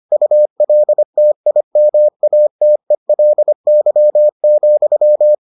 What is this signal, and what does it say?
Morse code, heard as a keyed tone, about 25 wpm, 600 Hz. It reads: ULTIMATELY,